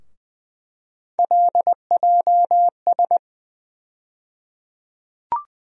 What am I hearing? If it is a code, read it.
Morse code, heard as a keyed tone, about 20 wpm, 700 Hz. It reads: LJS
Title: USA Call Sign Postfixes - 3 Letters 20wpm